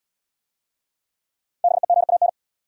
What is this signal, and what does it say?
Morse code, heard as a keyed tone, about 72 wpm, 690 Hz. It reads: BEGAN